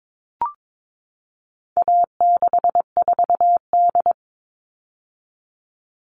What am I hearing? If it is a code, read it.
Morse code, heard as a keyed tone, about 22 wpm, 700 Hz. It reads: A64D
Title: All Cut Numbers - 4-Digit 22wpm